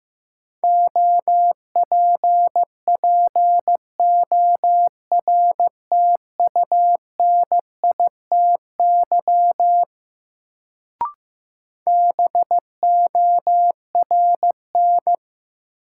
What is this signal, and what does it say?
Morse code, heard as a keyed tone, about 15 wpm, 700 Hz. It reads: OPPORTUNITY BORN
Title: Sentences from Top 1000 Words - New Word Drill - 15wpm